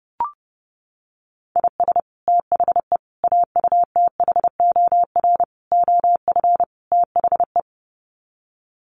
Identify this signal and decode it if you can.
Morse code, heard as a keyed tone, about 30 wpm, 700 Hz. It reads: IS THE AUTHOR OF THE
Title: Sets of 5 Words Encore 30wpm